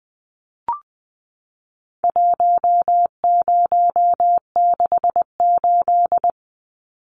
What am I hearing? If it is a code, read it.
Morse code, heard as a keyed tone, about 20 wpm, 700 Hz. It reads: 1068
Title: Numbers - 4-Digit 20wpm